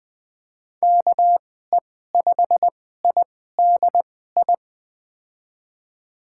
Morse code, heard as a keyed tone, about 20 wpm, spaced as for 14 wpm, 700 Hz. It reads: KE5IDI
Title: USA Call Signs - 6 Characters - 2x Character Spacing - 20wpm